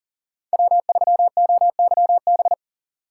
Morse code, heard as a keyed tone, about 40 wpm, 700 Hz. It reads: W3OYB
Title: USA Call Signs - 5 Characters - 3x Character Spacing - 40wpm